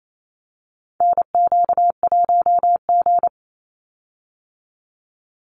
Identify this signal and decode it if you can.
Morse code, heard as a keyed tone, about 28 wpm, 700 Hz. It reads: NQ1G
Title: USA Call Signs 28wpm